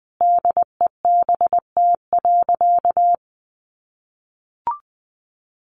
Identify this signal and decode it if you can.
Morse code, heard as a keyed tone, about 20 wpm, 700 Hz. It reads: DEBT.
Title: Sentences from Top 1300 Words 20wpm